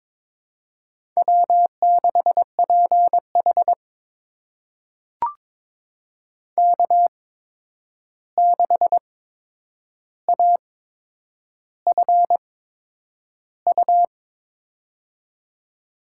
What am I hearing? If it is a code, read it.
Morse code, heard as a keyed tone, about 22 wpm, spaced as for 9 wpm, 700 Hz. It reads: W6PH K6AFU
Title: USA Call Signs - 8x Character Spacing - 22wpm